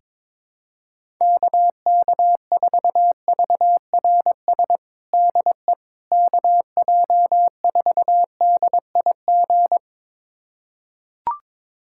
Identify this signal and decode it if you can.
Morse code, heard as a keyed tone, about 22 wpm, 700 Hz. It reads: KK4VRS DE KJ4DIG
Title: USA Callsign DE Callsign - 6 Characters - 22wpm